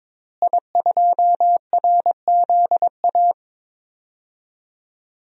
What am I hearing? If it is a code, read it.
Morse code, heard as a keyed tone, about 22 wpm, 700 Hz. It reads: I2RZA
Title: DXCC-100 Call Signs - 5 Characters - Repeated 2x - 22wpm